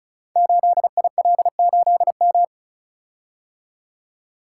Morse code, heard as a keyed tone, about 36 wpm, 690 Hz. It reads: 8IL8M
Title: Code Groups - 5 Letter:Numbers - 35wpm